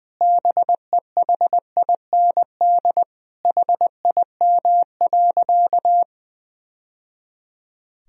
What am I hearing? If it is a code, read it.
Morse code, heard as a keyed tone, about 20 wpm, 700 Hz. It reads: BEHIND HIM.
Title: Sentences from Top 1000 Words - Review Part 1 - 20wpm